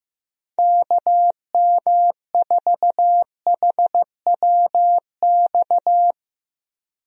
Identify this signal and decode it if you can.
Morse code, heard as a keyed tone, about 15 wpm, 700 Hz. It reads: KM4HWX
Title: USA Call Signs - 6 Characters - Repeated 2x - 15wpm